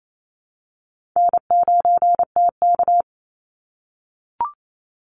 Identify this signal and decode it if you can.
Morse code, heard as a keyed tone, about 28 wpm, 700 Hz. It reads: N9TK
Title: Easy Call Signs - 28wpm